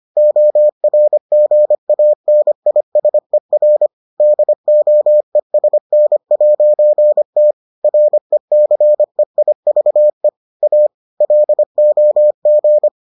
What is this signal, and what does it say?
Morse code, heard as a keyed tone, about 25 wpm, 600 Hz. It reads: ORGANISER DOESN'T RECEIVE A LOG